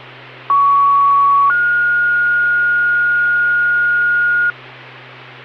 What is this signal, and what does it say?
Two-tone page, heard as a one-second tone, then a three-second tone, about 1120 then 1430 Hz, with noise. It is heard from 0.5 to 4.5 s.